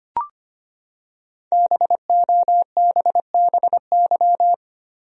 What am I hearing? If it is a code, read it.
Morse code, heard as a keyed tone, about 25 wpm, 700 Hz. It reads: BOBBY
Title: USA Common Names - Max 5 Letters - 25wpm